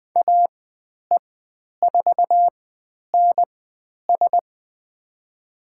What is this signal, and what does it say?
Morse code, heard as a keyed tone, about 20 wpm, spaced as for 10 wpm, 700 Hz. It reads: AE4NS
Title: 109 Lesson 18g - Letter m - Call Signs 20wpm